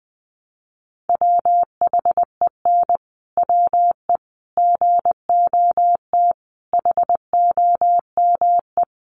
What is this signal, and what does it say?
Morse code, heard as a keyed tone, about 20 wpm, 700 Hz. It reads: WHEN WE GOT HOME
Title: Sentences from Top 500 Words 20wpm